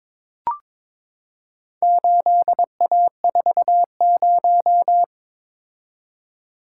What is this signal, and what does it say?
Morse code, heard as a keyed tone, about 22 wpm, 700 Hz. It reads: 8A40
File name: practice / 22wpm / Four Letter-Number Combinations - 22wpm